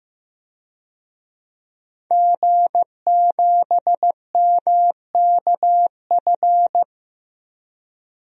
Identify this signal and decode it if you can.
Morse code, heard as a keyed tone, about 15 wpm, 700 Hz. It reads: G7MKF
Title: DXCC-100 Call Signs - 5 Characters - Repeated 2x - 15wpm